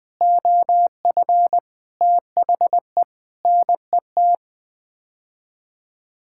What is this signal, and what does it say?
Morse code, heard as a keyed tone, about 21 wpm, 700 Hz. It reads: OF THE NET
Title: Sets of 8 Words - 20wpm